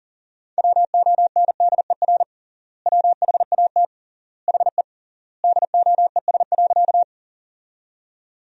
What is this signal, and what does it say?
Morse code, heard as a keyed tone, about 40 wpm, 700 Hz. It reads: WONDER WHAT HE DOES.
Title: 3x Word-Spacing - Sentences from Top 500 Words - 40wpm